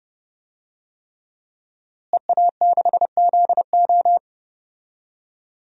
Morse code, heard as a keyed tone, about 32 wpm, 700 Hz. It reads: EA6ZO
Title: DXCC-100 Call Signs - 5 Characters - Repeated 3x - 30wpm